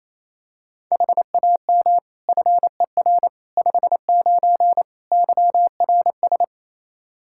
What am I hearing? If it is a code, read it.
Morse code, heard as a keyed tone, about 28 wpm, 700 Hz. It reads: HAM FER 59 YRS
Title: Practice QSO - Exchange 3 - More Info - 28wpm